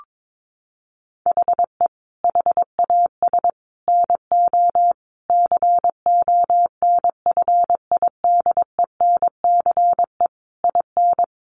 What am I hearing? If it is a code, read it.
Morse code, heard as a keyed tone, about 22 wpm, 700 Hz. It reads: HE HAS NO CONFIDENCE IN
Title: Sentences from Top 1500 Words 22wpm